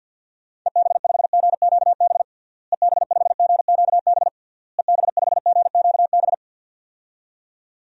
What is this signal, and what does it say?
Morse code, heard as a keyed tone, about 51 wpm, 700 Hz. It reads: EB5GQB EB5GQB EB5GQB
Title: DXCC-100 Call Signs - 6 Characters - Repeated 3x - 50wpm